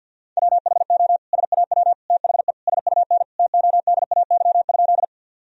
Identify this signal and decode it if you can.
Morse code, heard as a keyed tone, about 47 wpm, 700 Hz. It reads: WHO SAW THE SUN TODAY?
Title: Sentences from Top 1000 Words - Review Part 1 - 50wpm